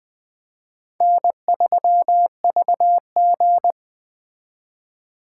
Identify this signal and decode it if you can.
Morse code, heard as a keyed tone, about 20 wpm, 700 Hz. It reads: N3VG
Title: Speed Racing - USA 1x2 Call Signs 17wpm